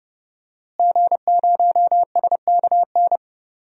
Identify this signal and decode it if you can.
Morse code, heard as a keyed tone, about 30 wpm, 700 Hz. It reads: G0SKN
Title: DXCC-100 Call Signs - 5 Characters - Repeated 2x - 30wpm